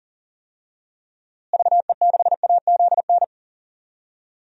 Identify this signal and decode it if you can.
Morse code, heard as a keyed tone, about 40 wpm, 700 Hz. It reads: VE6AZN
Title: DXCC-100 Call Signs - 6 Characters - Repeated 3x - 40wpm